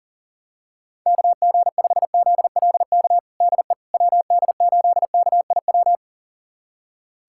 Morse code, heard as a keyed tone, about 40 wpm, 700 Hz. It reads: KG5ZLK DE WD8KIW
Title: USA Callsign DE Callsign - 6 Characters - 40wpm